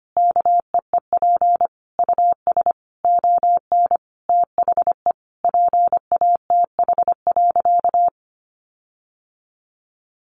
Morse code, heard as a keyed tone, about 25 wpm, 700 Hz. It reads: KEEP US ON THE PATH.
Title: Sentences from Top 1400 Words 25wpm